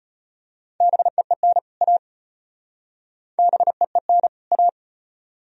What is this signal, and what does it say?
Morse code, heard as a keyed tone, about 33 wpm, 700 Hz. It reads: BEEN A BEEN A